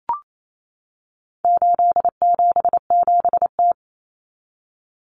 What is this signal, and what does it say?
Morse code, heard as a keyed tone, about 28 wpm, 700 Hz. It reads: 877T
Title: Common Cut Numbers - 4-Digit 28wpm